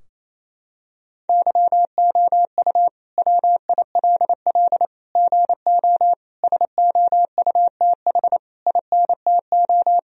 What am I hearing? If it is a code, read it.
Morse code, heard as a keyed tone, about 28 wpm, 700 Hz. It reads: YOU WILL GO SOUTH INTO
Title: Sentences from Top 1000 Words - Review Part 2 - 28wpm